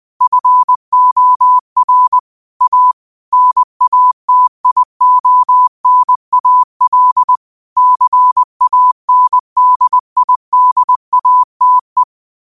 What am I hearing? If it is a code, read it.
Morse code, heard as a keyed tone, about 20 wpm, 1000 Hz. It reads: FOR A NATIONAL CANDIDATE
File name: qotdfiles / 20wpm